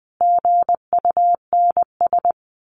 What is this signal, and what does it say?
Morse code, heard as a keyed tone, about 20 wpm, 700 Hz. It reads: GUNS
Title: A Sound Of Thunder 20wpm